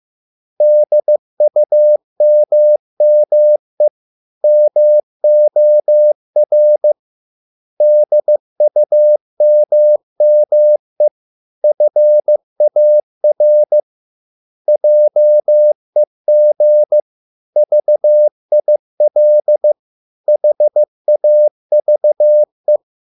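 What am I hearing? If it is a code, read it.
Morse code, heard as a keyed tone, about 15 wpm, 600 Hz. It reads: DUMME MOR DUMME FAR JEG VIL HAVE
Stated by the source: Morse code at a slow 15 wpm from CW med Gnister.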